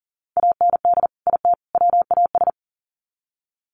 Morse code, heard as a keyed tone, about 40 wpm, 700 Hz. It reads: AND IT WAS
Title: Sets of 3 Words 40wpm